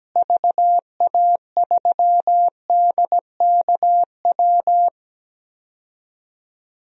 Morse code, heard as a keyed tone, about 17 wpm, 700 Hz. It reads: VA3DKW
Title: DXCC-100 Call Signs - 6 Characters - Repeated 3x - 17wpm